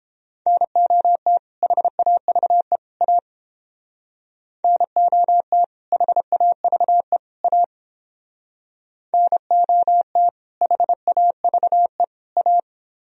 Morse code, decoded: NOT HAVE A NOT HAVE A NOT HAVE A

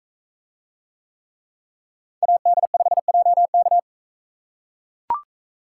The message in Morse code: AD5JK

